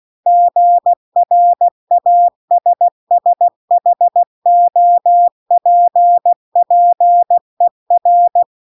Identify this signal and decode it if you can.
Morse code, heard as a keyed tone, about 16 wpm, 700 Hz. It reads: GRASSHOPPER